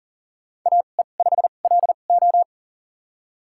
Morse code, heard as a keyed tone, about 40 wpm, spaced as for 29 wpm, 700 Hz. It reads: AE5LO
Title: USA Call Signs - 5 Characters - 2x Character Spacing - 40wpm